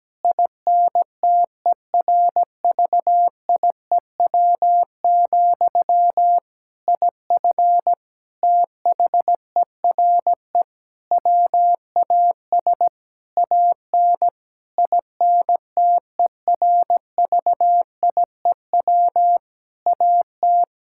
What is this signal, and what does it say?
Morse code, heard as a keyed tone, about 17 wpm, 700 Hz. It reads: INTERVIEW, IF THERE WAS AN INTERVIEW AT
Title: Sentences from Top 1600 Words 17wpm